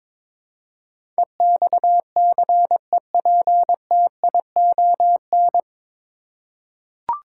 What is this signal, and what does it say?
Morse code, heard as a keyed tone, about 22 wpm, 700 Hz. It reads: EXCEPTION
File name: 9 Letter Words - First 3 Letters Given - 22wpm